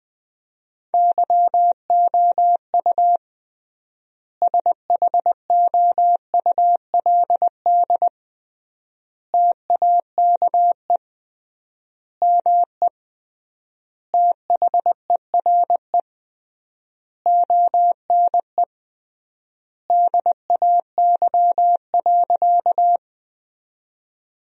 Morse code, heard as a keyed tone, about 20 wpm, 700 Hz. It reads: YOU SHOULD TAKE ME THERE ONE DAY.
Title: 3x Word-Spacing - Sentences from Top 300 Words - 20wpm